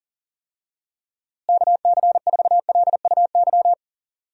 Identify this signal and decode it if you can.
Morse code, heard as a keyed tone, about 40 wpm, 700 Hz. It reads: KC4LUY
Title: USA Call Signs - 6 Characters - Repeated 3x - 40wpm